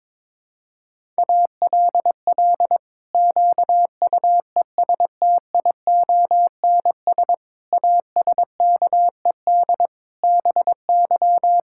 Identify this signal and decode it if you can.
Morse code, heard as a keyed tone, about 21 wpm, 700 Hz. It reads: ALL QUESTIONS ASKED BY
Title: Pangram Challenge v2 - 22wpm